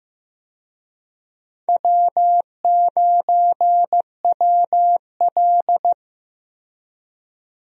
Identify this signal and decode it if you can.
Morse code, heard as a keyed tone, about 15 wpm, 700 Hz. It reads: W9WL
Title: USA Call Signs - 4 Characters - Repeated 3x - 15wpm